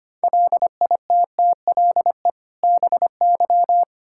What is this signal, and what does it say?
Morse code, heard as a keyed tone, about 25 wpm, 700 Hz. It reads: LITTLE BY
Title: Sentences from Top 1600 Words 25wpm